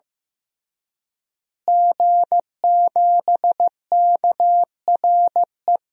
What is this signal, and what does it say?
Morse code, heard as a keyed tone, about 15 wpm, 700 Hz. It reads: G7KRE
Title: DXCC-100 Call Signs - 5 Characters - Repeated 3x - 15wpm